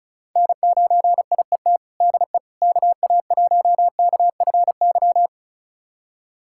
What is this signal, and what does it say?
Morse code, heard as a keyed tone, about 34 wpm, 700 Hz. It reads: N9IET DE KA1KFY
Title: USA Callsign DE Callsign 35wpm